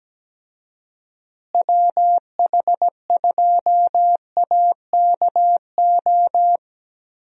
Morse code, heard as a keyed tone, about 17 wpm, 700 Hz. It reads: WH2AKO
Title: USA Call Signs - 6 Characters - Repeated 2x - 17wpm